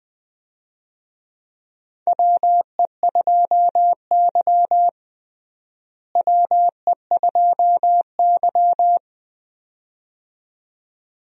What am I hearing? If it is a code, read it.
Morse code, heard as a keyed tone, about 20 wpm, 700 Hz. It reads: WE2Y WE2Y
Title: USA Call Signs - 4 Characters - Repeated 2x - 20wpm